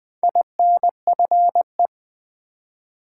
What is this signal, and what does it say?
Morse code, heard as a keyed tone, about 20 wpm, 700 Hz. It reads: INFE
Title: Build A Word - 2 Letters At A Time - 6-Letter Words - 20wpm